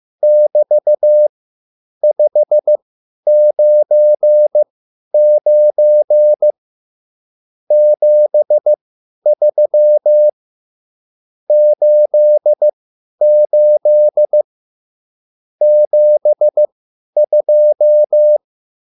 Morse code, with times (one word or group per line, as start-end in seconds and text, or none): 0.2-6.5 <BT>599
7.7-10.3 73
11.5-14.4 88
15.6-18.4 72